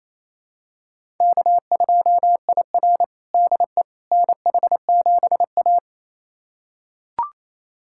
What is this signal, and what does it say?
Morse code, heard as a keyed tone, about 27 wpm, 700 Hz. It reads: K2IR DE NH7A
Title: USA Callsign DE Callsign - 4 Characters - 28wpm